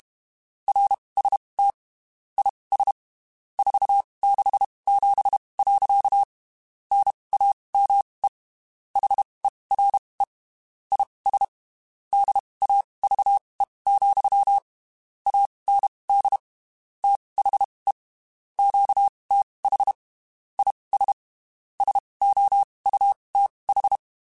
Morse code, heard as a keyed tone, about 32 wpm, spaced as for 23 wpm, 800 Hz. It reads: RST IS 467. NAME HERE IS DAVE, AND THE QTH IS SOUTH
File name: (spaced) / QSOs Spaced-mp3s / WA7QOX 2 @30 sp